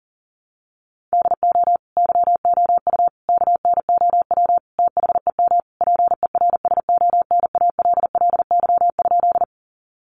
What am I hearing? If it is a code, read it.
Morse code, heard as a keyed tone, about 40 wpm, 700 Hz. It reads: DO YOU KNOW THEM PERSONALLY?
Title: Sentences from Top 1400 Words 40wpm